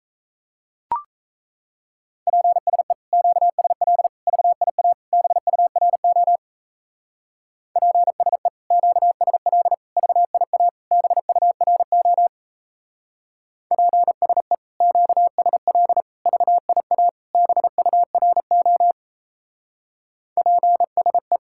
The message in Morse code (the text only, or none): PSE QSL VIA BURO PSE QSL VIA BURO PSE QSL VIA BURO PSE